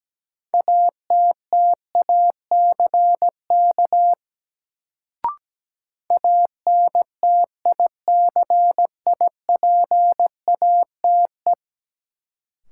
Morse code, heard as a keyed tone, about 17 wpm, 700 Hz. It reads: ATTACK ANTICIPATE
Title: Sentences from Top 1500 Words - New Word Drill - 17wpm